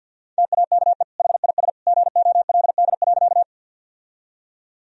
Morse code, heard as a keyed tone, about 50 wpm, 700 Hz. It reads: TAKE HIS GOLD.
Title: Sentences from Top 1000 Words - Review Part 3 - 50wpm